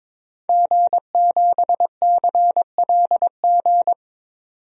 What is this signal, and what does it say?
Morse code, heard as a keyed tone, about 22 wpm, 700 Hz. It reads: G7CLG
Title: DXCC-100 Call Signs - 5 Characters - 22wpm